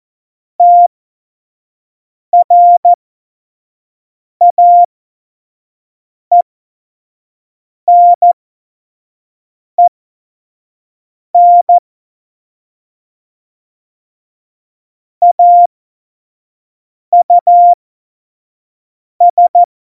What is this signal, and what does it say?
Morse code, heard as a keyed tone, about 14 wpm, spaced as for 5 wpm, 700 Hz. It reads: TRAENEN AUS
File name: Morsekurs / Witze / Witz 24 April 2018, 25 F70